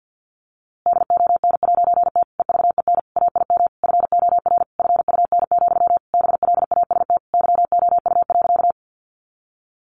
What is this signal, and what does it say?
Morse code, heard as a keyed tone, about 50 wpm, 700 Hz. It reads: DON'T EVEN AIM FOR FUN, BLAST YOU.
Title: A Sound Of Thunder 50wpm